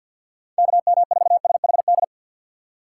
Morse code, heard as a keyed tone, about 48 wpm, 700 Hz. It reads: KM4SHD
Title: USA Call Signs - 3x Character Spacing - 50wpm